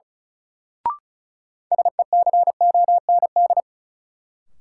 Morse code, heard as a keyed tone, about 34 wpm, 700 Hz. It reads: SECOND
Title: Sentences from Top 300 Words - New Word Drill - 35wpm